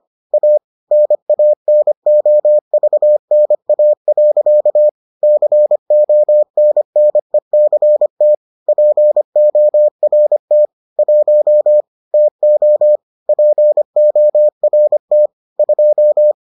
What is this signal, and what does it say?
Morse code, heard as a keyed tone, about 25 wpm, 600 Hz. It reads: A NANOVNA. CONNECT PORT 1 TO PORT 2